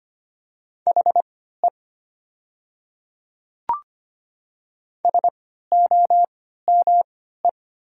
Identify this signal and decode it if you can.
Morse code, heard as a keyed tone, about 25 wpm, spaced as for 14 wpm, 700 Hz. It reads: HE SOME
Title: 3x Character Spacing - Top 100 Words - 25wpm